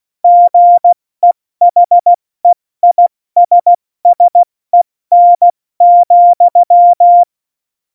Morse code, heard as a keyed tone, about 16 wpm, spaced as for 14 wpm, 700 Hz. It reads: GEHEISSEN,
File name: Morsekurs / Witze / Witz 08. Dezember 2020, 70 F80